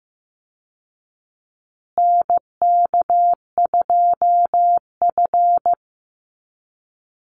Morse code, heard as a keyed tone, about 15 wpm, 700 Hz. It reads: NK2F
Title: Call Signs Repeated 3x - 15wpm